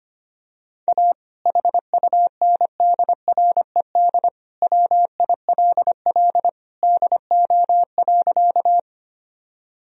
Morse code, heard as a keyed tone, about 25 wpm, 700 Hz. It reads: A HUNDRED WILL DO.
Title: Sentences from Top 1000 Words - Review Part 2 - 25wpm